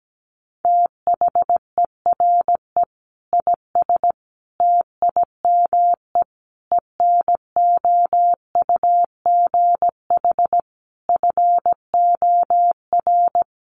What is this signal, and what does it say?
Morse code, heard as a keyed tone, about 17 wpm, 700 Hz. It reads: THERE IS TIME ENOUGH FOR
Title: Sentences from Top 1200 Words 17wpm